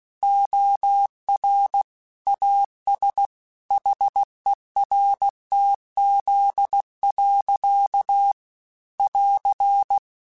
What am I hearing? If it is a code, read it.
Morse code, heard as a keyed tone, about 16 wpm, 800 Hz. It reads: OR AS HERTZ. <AR>
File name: Technical Subjects / Technical Subjects-mp3s / Reactance @15